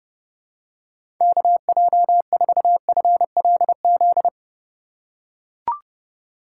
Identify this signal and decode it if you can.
Morse code, heard as a keyed tone, about 30 wpm, 700 Hz. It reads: KJ4FLZ
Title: USA Call Signs - 4x Character Spacing - 30wpm